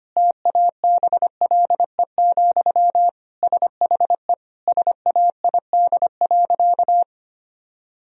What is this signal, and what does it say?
Morse code, heard as a keyed tone, about 25 wpm, 700 Hz. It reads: TABLE, SHE SAID.
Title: Sentences from Top 1000 Words - Review Part 2 - 25wpm